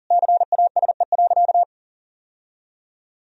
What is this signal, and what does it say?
Morse code, heard as a keyed tone, about 40 wpm, 700 Hz. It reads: CASE.
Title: Sentences from Top 1000 Words - Review Part 3 - 40wpm